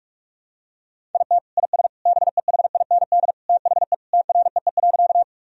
Morse code, heard as a keyed tone, about 43 wpm, 700 Hz. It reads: IT IS BEHIND THE TREE.
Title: Sentences from Top 1000 Words - Review Part 1 - 45wpm